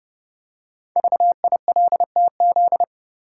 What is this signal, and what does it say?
Morse code, heard as a keyed tone, about 29 wpm, 690 Hz. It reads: VILTZ
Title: Code Groups - 5 Letters - 30wpm